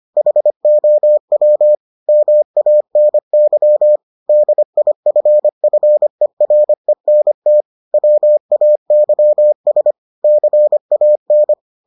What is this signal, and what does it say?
Morse code, heard as a keyed tone, about 25 wpm, 600 Hz. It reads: HOW MANY DIFFERENT WAYS CAN